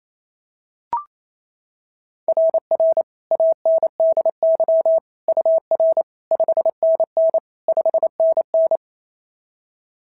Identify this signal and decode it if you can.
Morse code, heard as a keyed tone, about 25 wpm, 650 Hz. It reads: RR ANDY UR 5NN 5NN
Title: SOTA QSO Practice 28wpm